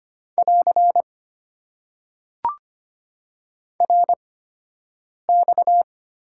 Morse code, heard as a keyed tone, about 25 wpm, spaced as for 7 wpm, 700 Hz. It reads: <AR> RX
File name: Top 90 QSO Elements - 8x Character Spacing - 25wpm